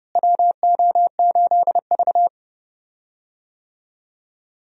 Morse code, heard as a keyed tone, about 30 wpm, 700 Hz. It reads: WO8V
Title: USA Call Signs - 4 Characters - Repeated 2x - 30wpm